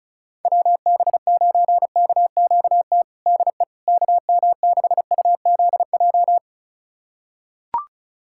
Morse code, heard as a keyed tone, about 34 wpm, 700 Hz. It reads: WB9KQT DE KM6UZJ
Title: USA Callsign DE Callsign - 6 Characters - 35wpm